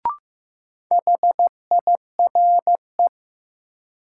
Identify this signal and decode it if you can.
Morse code, heard as a keyed tone, about 15 wpm, 700 Hz. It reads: HIRE
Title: Sentences from Top 1400 Words - New Word Drill Alt - 15wpm